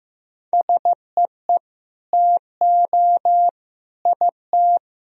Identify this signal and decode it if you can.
Morse code, heard as a keyed tone, about 15 wpm, 700 Hz. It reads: SEE TO IT
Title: Sentences from Top 1000 Words - Review Part 2 - 15wpm